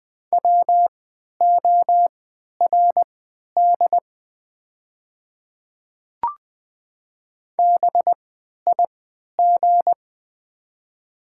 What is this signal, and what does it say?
Morse code, heard as a keyed tone, about 20 wpm, spaced as for 11 wpm, 700 Hz. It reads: WORD BIG
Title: 3x Character Spacing - Top 500 Words - 20wpm